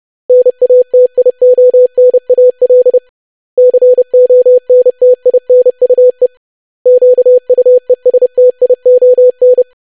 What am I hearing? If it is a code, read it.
Morse code, heard as a keyed tone, about 30 wpm, 500 Hz. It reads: NATIONAL CONTINUE QUESTION
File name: lesson / web / basic500-25wpm